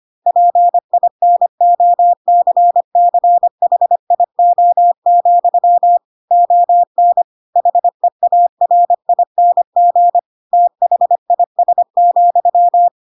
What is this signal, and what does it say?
Morse code, heard as a keyed tone, about 25 wpm, 700 Hz. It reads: PINOCCHIO, ON HEARING THIS,
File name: Pinocchio / Pinocchio ch3 25 25 ik1hge